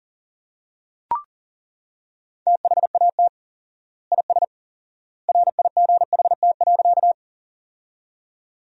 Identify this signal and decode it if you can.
Morse code, heard as a keyed tone, about 40 wpm, 700 Hz. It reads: THAT IS RIGHT.
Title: 4x Word-Spacing - Sentences from Top 200 Words - 40wpm